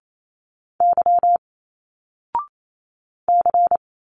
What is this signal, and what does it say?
Morse code, heard as a keyed tone, about 28 wpm, 700 Hz. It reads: Y C
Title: Single Letters 28wpm